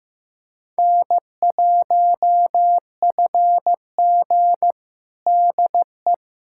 Morse code, heard as a keyed tone, about 16 wpm, 700 Hz. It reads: N1FG DE
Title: USA Callsign DE Callsign - 4 Characters - 15wpm